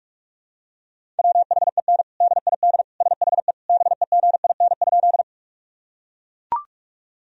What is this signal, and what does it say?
Morse code, heard as a keyed tone, about 41 wpm, 700 Hz. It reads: WHEN DID SHE BEGIN?
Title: Sentences from Top 1000 Words - Review Part 1 - 45wpm